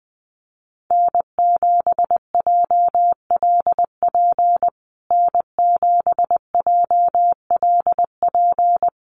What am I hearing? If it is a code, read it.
Morse code, heard as a keyed tone, about 19 wpm, 700 Hz. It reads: N7JLP N7JLP
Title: USA Call Signs Repeated 2x 20wpm